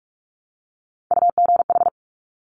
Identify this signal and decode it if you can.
Morse code, heard as a keyed tone, about 46 wpm, 690 Hz. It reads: UGH
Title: Top Three Letter Combinations in Words 45wpm